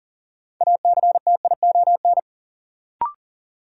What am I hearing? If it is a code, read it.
Morse code, heard as a keyed tone, about 40 wpm, 700 Hz. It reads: ACTION
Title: Sentences from Top 700 Words - New Word Drill - 40wpm